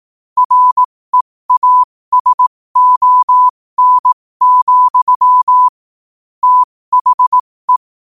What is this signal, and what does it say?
Morse code, heard as a keyed tone, about 18 wpm, spaced as for 15 wpm, 1000 Hz. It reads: REASON, THE